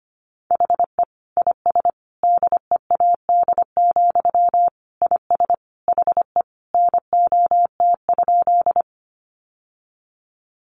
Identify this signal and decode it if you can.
Morse code, heard as a keyed tone, about 25 wpm, 700 Hz. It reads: HE IS DEAD, IS HE NOT?
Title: Sentences from Top 1500 Words 25wpm